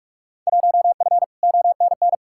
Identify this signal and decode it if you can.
Morse code, heard as a keyed tone, about 46 wpm, 700 Hz. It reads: 1F ONN
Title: Field Day Exchanges Encore - 45wpm